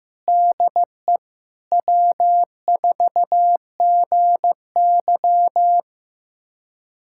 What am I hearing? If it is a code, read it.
Morse code, heard as a keyed tone, about 15 wpm, 700 Hz. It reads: DE W4GY
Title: USA Callsign DE Callsign - 4 Characters - 15wpm